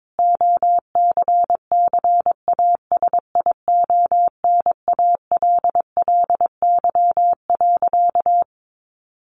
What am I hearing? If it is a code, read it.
Morse code, heard as a keyed tone, about 22 wpm, 700 Hz. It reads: OCCASIONALLY.
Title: Sentences from Top 1300 Words 22wpm